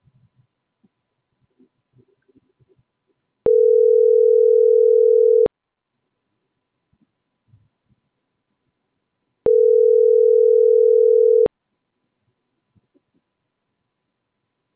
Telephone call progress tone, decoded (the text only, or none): ringback tone